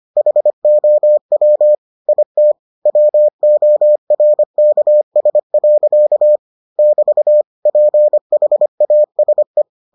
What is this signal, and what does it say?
Morse code, heard as a keyed tone, about 25 wpm, 600 Hz. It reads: HOW IT WORKS. <BT> PHASE